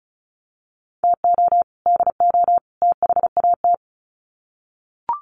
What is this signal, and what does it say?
Morse code, heard as a keyed tone, about 34 wpm, 700 Hz. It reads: TO DO THAT
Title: Sets of 3 Words 35wpm